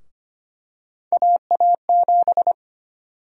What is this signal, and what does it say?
Morse code, heard as a keyed tone, about 25 wpm, 700 Hz. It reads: AA7
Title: USA Call Sign Prefixes - Repeated 2x - 25wpm